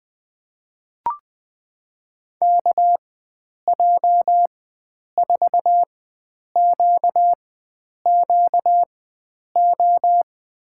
Morse code, heard as a keyed tone, about 20 wpm, spaced as for 9 wpm, 700 Hz. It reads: KJ4QQO
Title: USA Call Signs - 6 Characters - 4x Character Spacing - 20wpm